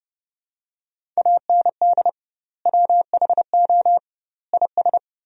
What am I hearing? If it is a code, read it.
Morse code, heard as a keyed tone, about 30 wpm, 700 Hz. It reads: AND WHO IS
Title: 2x Word-Spacing - Sentences from Top 100 Words - 30wpm